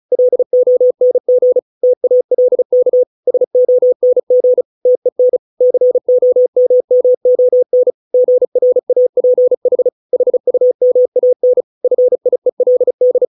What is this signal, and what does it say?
Morse code, heard as a keyed tone, about 34 wpm, 500 Hz. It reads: LONG TALK SONG TEN COMMON GRAPH HUMAN FIELD